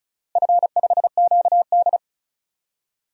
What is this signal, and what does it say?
Morse code, heard as a keyed tone, about 36 wpm, 700 Hz. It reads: F5QD
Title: DXCC-100 Call Signs - 4 Characters - Repeated 3x - 35wpm